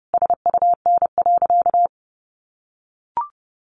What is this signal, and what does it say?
Morse code, heard as a keyed tone, about 30 wpm, 700 Hz. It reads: SUN.
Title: A Sound Of Thunder 30wpm